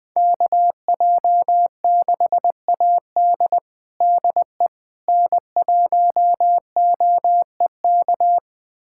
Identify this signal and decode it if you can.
Morse code, heard as a keyed tone, about 20 wpm, 700 Hz. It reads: KJ6AD DE N1OEK
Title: USA Callsign DE Callsign - 5 Characters - 20wpm